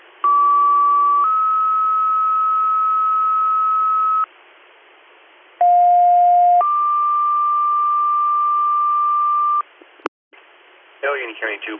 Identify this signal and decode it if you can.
Two-tone page, heard as a one-second tone, then a three-second tone, about 1150 then 1290 Hz, about 710 then 1180 Hz, with speech. It is heard from 0.2 to 9.6 s.